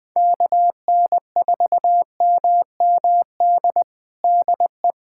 Morse code, heard as a keyed tone, about 20 wpm, 700 Hz. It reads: KN4MMD DE
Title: USA Callsign DE Callsign - 6 Characters - 20wpm